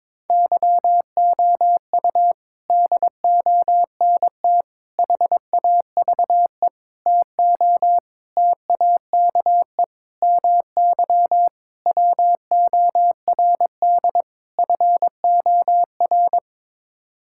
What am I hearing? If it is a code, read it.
Morse code, heard as a keyed tone, about 22 wpm, 700 Hz. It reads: YOU DONT HAVE TO TAKE MY WORD FOR
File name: practice / 22wpm / Sets of 8 Words - 22wpm